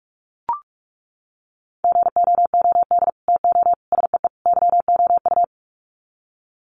Morse code, heard as a keyed tone, about 44 wpm, 700 Hz. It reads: GOOD TO SEE YOU
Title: Sets of 4 Words 45wpm